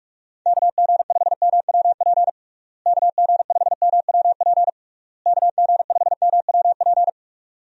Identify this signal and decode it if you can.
Morse code, heard as a keyed tone, about 44 wpm, 700 Hz. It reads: KG5MWP KG5MWP KG5MWP
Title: USA Call Signs - 6 Characters - Repeated 3x - 45wpm